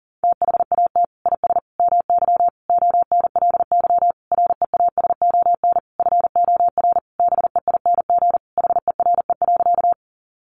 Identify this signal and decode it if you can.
Morse code, heard as a keyed tone, about 40 wpm, 700 Hz. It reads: THAT IS MY ONLY REASON FOR BEING HERE.
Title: Sentences from Top 700 Words 40wpm